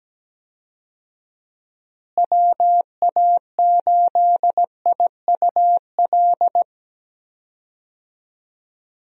Morse code, heard as a keyed tone, about 17 wpm, 700 Hz. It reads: WA8IUL
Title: USA Call Signs - 6 Characters - Repeated 3x - 17wpm